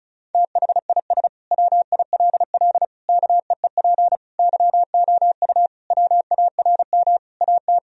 Morse code, heard as a keyed tone, about 34 wpm, 700 Hz. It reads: THIS WILL KEEP YOU WARM AT
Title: Sentences from Top 1000 Words - Review Part 2 - 35wpm